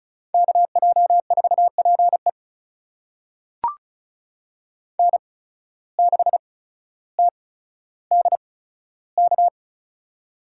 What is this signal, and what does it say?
Morse code, heard as a keyed tone, about 34 wpm, spaced as for 15 wpm, 700 Hz. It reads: KJ4PE N6TDK
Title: USA Call Signs - 5 Characters - 8x Character Spacing - 35wpm